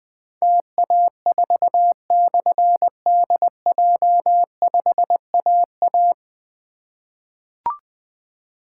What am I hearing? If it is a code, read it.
Morse code, heard as a keyed tone, about 21 wpm, 700 Hz. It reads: TA4/DJ5AA
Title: Difficult Call Signs 20wpm